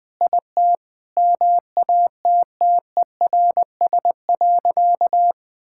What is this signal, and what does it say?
Morse code, heard as a keyed tone, about 20 wpm, 700 Hz. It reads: IT MATTERS.
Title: Sentences with 5-Word Phrases - 20wpm